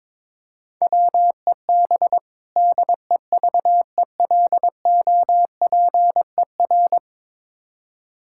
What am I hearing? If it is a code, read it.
Morse code, heard as a keyed tone, about 23 wpm, 700 Hz. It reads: WEB DEVELOPER
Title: K1USN SST QSO Practice 22wpm